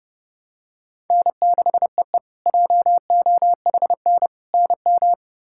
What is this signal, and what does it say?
Morse code, heard as a keyed tone, about 30 wpm, 700 Hz. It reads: N6EE JOHN NM
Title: K1USN SST QSO Practice - 30wpm